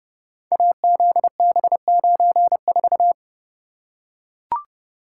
Morse code, heard as a keyed tone, about 30 wpm, 700 Hz. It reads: AZB94
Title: Code Groups - 5 Letter:Numbers - 30wpm